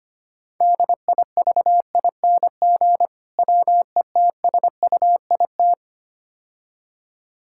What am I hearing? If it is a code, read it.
Morse code, heard as a keyed tone, about 25 wpm, 700 Hz. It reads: DIVING WETSUIT
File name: practice / 25wpm / Difficult Words - Speed-Racing with Hints - 25wpm